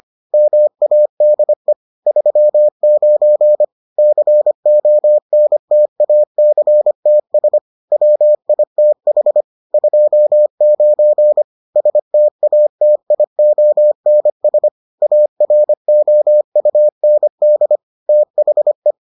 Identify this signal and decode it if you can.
Morse code, heard as a keyed tone, about 25 wpm, 600 Hz. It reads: MADE 39 CONTACTS WITH 29 STATIONS AROUND THE